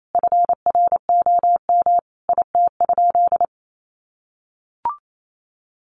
Morse code, heard as a keyed tone, about 28 wpm, 700 Hz. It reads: FROM IT?
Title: Sentences from Top 800 Words 28wpm